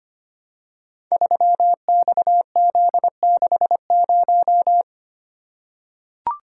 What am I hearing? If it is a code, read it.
Morse code, heard as a keyed tone, about 25 wpm, 700 Hz. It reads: 3XZ60
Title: Code Groups - 5 Letter:Numbers - 25wpm